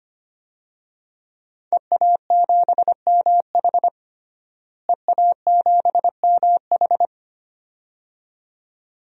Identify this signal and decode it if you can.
Morse code, heard as a keyed tone, about 25 wpm, 700 Hz. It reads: EA7MH EA7MH
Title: DXCC-100 Call Signs - 5 Characters - Repeated 2x - 25wpm